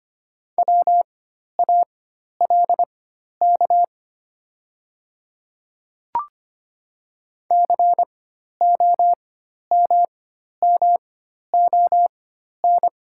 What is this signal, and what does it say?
Morse code, heard as a keyed tone, about 25 wpm, spaced as for 12 wpm, 700 Hz. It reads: WALK COMMON
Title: 4x Character Spacing - Top 500 Words - 25wpm